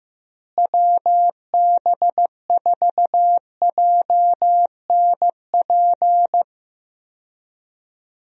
Morse code, heard as a keyed tone, about 15 wpm, 700 Hz. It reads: WB4JNP
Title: USA Call Signs - 6 Characters - Repeated 3x - 15wpm